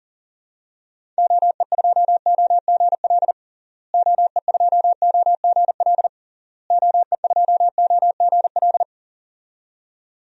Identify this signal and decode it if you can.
Morse code, heard as a keyed tone, about 40 wpm, 700 Hz. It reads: OE2OGL OE2OGL OE2OGL
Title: DXCC-100 Call Signs - 6 Characters - Repeated 3x - 40wpm